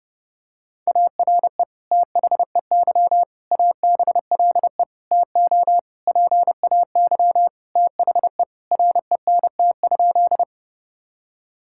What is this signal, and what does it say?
Morse code, heard as a keyed tone, about 30 wpm, 700 Hz. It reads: ARE THEY ABLE TO PAY THE RENT?
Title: Sentences from Top 1000 Words - Review Part 5 - 30wpm